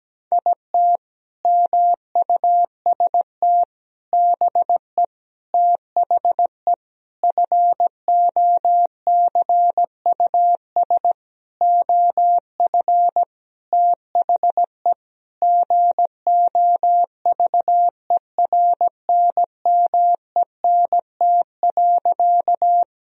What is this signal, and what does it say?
Morse code, heard as a keyed tone, about 17 wpm, 700 Hz. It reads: IT MUST BE THE FOCUS OF THE GOVERNMENT.
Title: Sentences from Top 1000 Words - Review Part 4 - 17wpm